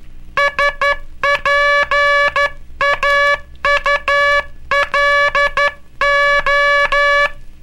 Morse code, heard as a keyed tone, about 12 wpm, 1700 Hz. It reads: SPAULO
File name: cursos / Curso CW mp3 / CW Aula - 18